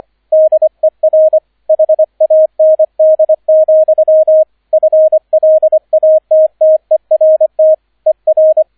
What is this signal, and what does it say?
Morse code, heard as a keyed tone, about 25 wpm, 630 Hz, with radio noise. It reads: DER HAND, FLATTERT ER
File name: 003 Dritter Streich 25wpm